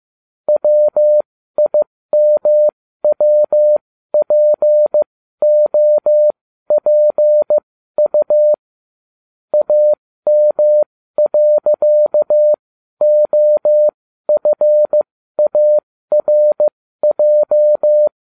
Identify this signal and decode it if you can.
Morse code, heard as a keyed tone, about 15 wpm, spaced as for 12 wpm, 600 Hz. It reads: WIMWPOPU AM.OFARJ